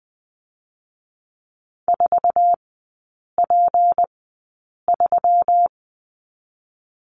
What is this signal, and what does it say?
Morse code, heard as a keyed tone, about 20 wpm, spaced as for 8 wpm, 700 Hz. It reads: 4P3
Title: 166 Lesson 27e - Number 7 - Three Characters 20wpm